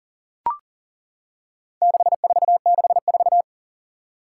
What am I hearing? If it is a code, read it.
Morse code, heard as a keyed tone, about 40 wpm, 700 Hz. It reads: 6464
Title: Four Challenging Letter-Number Combinations 40wpm